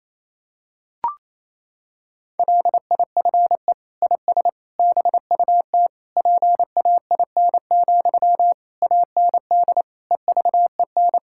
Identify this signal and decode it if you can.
Morse code, heard as a keyed tone, about 28 wpm, 700 Hz. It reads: LIFE IS BUT PAIN, AND EVEN
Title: Sentences from Top 1600 Words 28wpm